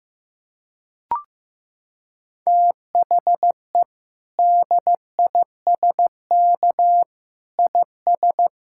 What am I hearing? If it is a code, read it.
Morse code, heard as a keyed tone, about 15 wpm, 700 Hz. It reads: THE DISK IS
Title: Sentences from Top 1000 Words - Review Part 6 - 15wpm